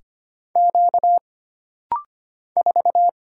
Morse code, heard as a keyed tone, about 25 wpm, 700 Hz. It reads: Q 4
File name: Single Character (Letter:Number:Punctuation:Prosign) - Rapid-Fire - 25wpm